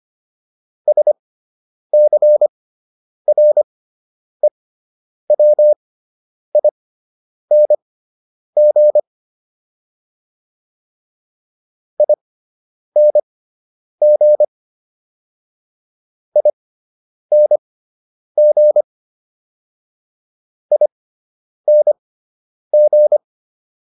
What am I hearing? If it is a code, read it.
Morse code, heard as a keyed tone, about 25 wpm, spaced as for 9 wpm, 600 Hz. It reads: SCREWING ING ING ING
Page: fundamental-practice-files